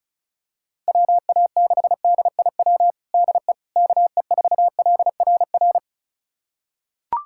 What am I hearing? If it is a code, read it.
Morse code, heard as a keyed tone, about 36 wpm, 700 Hz. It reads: WA6DIW DE KE4LRR
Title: USA Callsign DE Callsign 35wpm